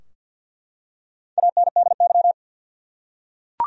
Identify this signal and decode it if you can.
Morse code, heard as a keyed tone, about 51 wpm, 700 Hz. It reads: ANDY
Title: USA Common Names - 50wpm